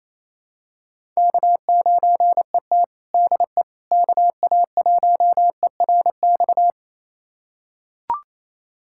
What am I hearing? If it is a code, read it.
Morse code, heard as a keyed tone, about 28 wpm, 700 Hz. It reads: K9ET DE KA1ERX
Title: USA Callsign DE Callsign 28wpm